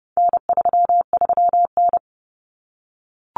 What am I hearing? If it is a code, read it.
Morse code, heard as a keyed tone, about 30 wpm, 700 Hz. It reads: N33N
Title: Common Cut Numbers - 4-Digit 30wpm